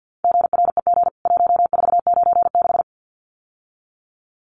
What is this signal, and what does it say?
Morse code, heard as a keyed tone, about 48 wpm, 700 Hz. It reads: GREG 1496
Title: CWT Exchange 50wpm